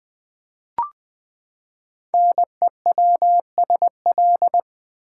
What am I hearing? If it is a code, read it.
Morse code, heard as a keyed tone, about 20 wpm, 700 Hz. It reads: NEWSL
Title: Build A Word - 5 Letters At A Time - 10-Letter Words - 20wpm